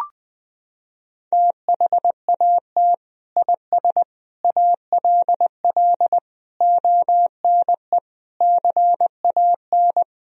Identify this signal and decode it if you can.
Morse code, heard as a keyed tone, about 20 wpm, 700 Hz. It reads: THAT IS ALL ONE CAN
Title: Sentences from Top 200 Words and Abbreviations 20wpm